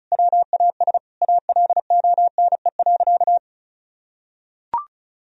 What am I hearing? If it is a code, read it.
Morse code, heard as a keyed tone, about 36 wpm, 700 Hz. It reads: WAS ALONE.
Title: Sentences from Top 1000 Words - Review Part 1 - 35wpm